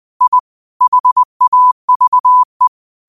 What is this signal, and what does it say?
Morse code, heard as a keyed tone, about 20 wpm, 1000 Hz. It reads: I HAVE
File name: qotdfiles / 20wpm